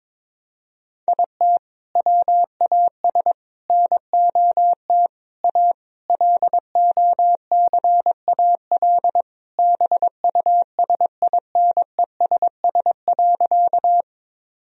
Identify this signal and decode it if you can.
Morse code, heard as a keyed tone, about 21 wpm, 700 Hz. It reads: IT WAS NOT A LOCAL BUSINESS.
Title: Sentences from Top 1000 Words - Review Part 3 - 22wpm